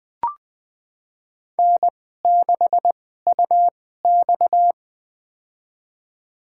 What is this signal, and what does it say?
Morse code, heard as a keyed tone, about 20 wpm, spaced as for 15 wpm, 700 Hz. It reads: N6UX
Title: USA Call Signs - 4 Characters - 2x Character Spacing - 20wpm